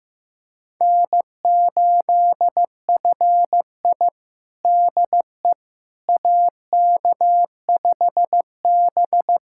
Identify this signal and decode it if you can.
Morse code, heard as a keyed tone, about 15 wpm, 700 Hz. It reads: N8FI DE AK5B
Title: USA Callsign DE Callsign - 4 Characters - 15wpm